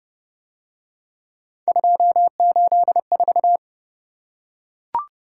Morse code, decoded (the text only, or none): 284